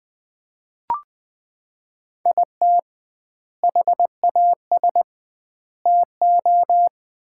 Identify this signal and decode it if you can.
Morse code, heard as a keyed tone, about 21 wpm, 700 Hz. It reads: IT HAS TO
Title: 2x Word-Spacing - Sets of 5 Words - 20wpm